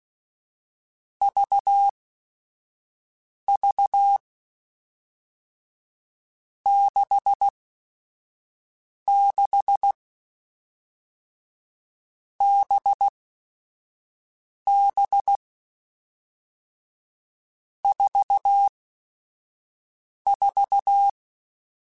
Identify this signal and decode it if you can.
Morse code, heard as a keyed tone, about 16 wpm, spaced as for 5 wpm, 800 Hz. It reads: VV 66 BB 44
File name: Groups@5 / Groups@5 mp3s / VB46@5